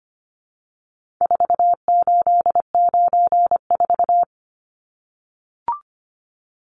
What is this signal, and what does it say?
Morse code, heard as a keyed tone, about 25 wpm, 700 Hz. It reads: 4894
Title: Numbers - 4-Digit 25wpm